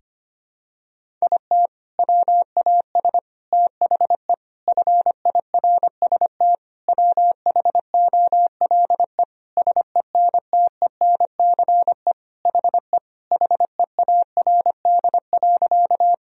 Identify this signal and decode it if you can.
Morse code, heard as a keyed tone, about 25 wpm, 700 Hz. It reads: IT WAS THE FIRST WHOLE SENTENCE HE HEARD.
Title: Sentences from Top 1000 Words - Review Part 1 - 25wpm